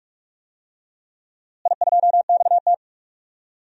Morse code, heard as a keyed tone, about 46 wpm, 700 Hz. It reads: I2XT
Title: DXCC-100 Call Signs - 4 Characters - Repeated 2x - 45wpm